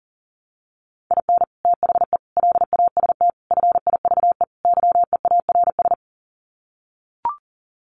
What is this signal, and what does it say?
Morse code, heard as a keyed tone, about 40 wpm, 700 Hz. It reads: IN THE LAST FIVE YEARS
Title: Sets of 5 Words Encore 40wpm